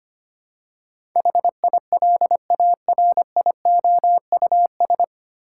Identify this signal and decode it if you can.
Morse code, heard as a keyed tone, about 25 wpm, 700 Hz. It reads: HILARIOUS
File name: Nine Letter Words - 25wpm